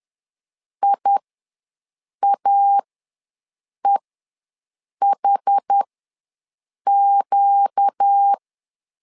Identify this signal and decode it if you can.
Morse code, heard as a keyed tone, about 10 wpm, spaced as for 6 wpm, 800 Hz. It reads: IAEHQ
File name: LEZIONI / 4 Q ws 10wpm 2306